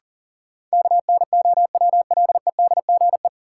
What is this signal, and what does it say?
Morse code, heard as a keyed tone, about 40 wpm, 700 Hz. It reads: KNOWLEDGE
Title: Sentences from Top 600 Words - New Word Drill - 40wpm